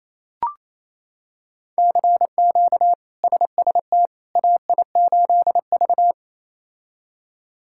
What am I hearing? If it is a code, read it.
Morse code, heard as a keyed tone, about 28 wpm, 700 Hz. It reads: CQ SST AI8V
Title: K1USN SST QSO Practice - 28wpm